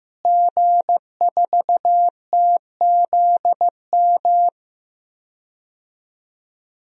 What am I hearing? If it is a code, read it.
Morse code, heard as a keyed tone, about 15 wpm, 700 Hz. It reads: G4TZM
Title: DXCC-100 Call Signs - 5 Characters - Repeated 2x - 15wpm